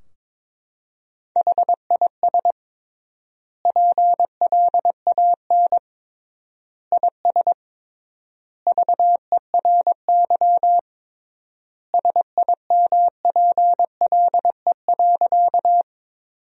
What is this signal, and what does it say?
Morse code, heard as a keyed tone, about 22 wpm, 700 Hz. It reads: HIS PLAN IS VERY SIMPLE.
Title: 3x Word-Spacing - Sentences from Top 500 Words - 22wpm